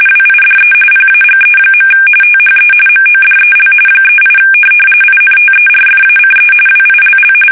RTTY, about 45.45 baud. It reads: WIDE PAGE. THE MODEL 38 INTERFACE WILL
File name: fsk850 / teletype-corp